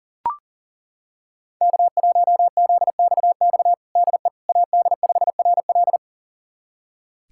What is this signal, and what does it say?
Morse code, heard as a keyed tone, about 37 wpm, 700 Hz. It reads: K1ZXX DE AD5RL
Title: USA Callsign DE Callsign - 5 Characters - 40wpm